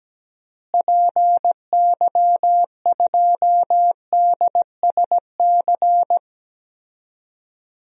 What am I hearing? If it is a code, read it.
Morse code, heard as a keyed tone, about 17 wpm, 700 Hz. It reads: PY2DSC
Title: DXCC-100 Call Signs - 6 Characters - Repeated 2x - 17wpm